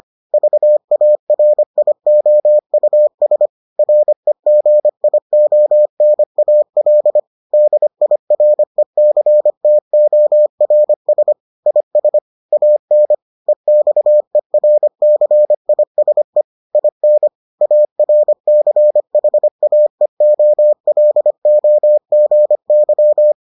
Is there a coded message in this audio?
Yes: Morse code from 0.3 to 23.3 s, heard as a keyed tone, about 25 wpm, 600 Hz. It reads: VARIOUS REGIONAL DIRECTORS IS AN EXERCISE IN ARCHAEOLOGY